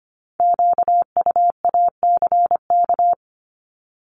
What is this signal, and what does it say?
Morse code, heard as a keyed tone, about 25 wpm, 700 Hz. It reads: QUACK
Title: Multi-letter Phonemes Repeated 3x 25wpm